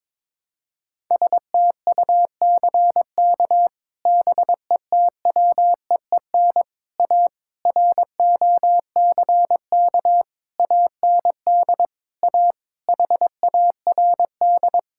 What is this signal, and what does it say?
Morse code, heard as a keyed tone, about 22 wpm, 700 Hz. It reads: STUCK BETWEEN A ROCK AND A HARD
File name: practice / 22wpm / Sets of 8 Words - 22wpm